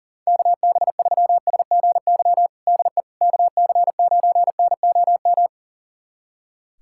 Morse code, heard as a keyed tone, about 40 wpm, 700 Hz. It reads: KB3SGY DE KC9NOM
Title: USA Callsign DE Callsign - 6 Characters - 40wpm